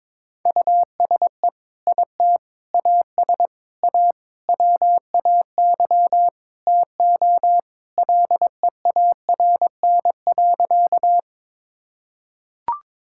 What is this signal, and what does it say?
Morse code, heard as a keyed tone, about 22 wpm, 700 Hz. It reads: USE IT AS A WAY TO LEARN.
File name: practice / 22wpm / Sentences with 4-Word Phrases - 22wpm